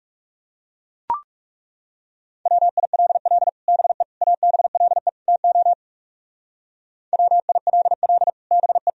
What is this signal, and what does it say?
Morse code, heard as a keyed tone, about 34 wpm, 700 Hz. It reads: WILL BE ABLE TO WILL BE
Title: Speed-Racing - Sets of 4 Words 30wpm